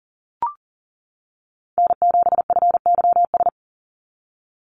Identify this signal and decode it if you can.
Morse code, heard as a keyed tone, about 37 wpm, 700 Hz. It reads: N7FYS
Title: USA Call Signs 40wpm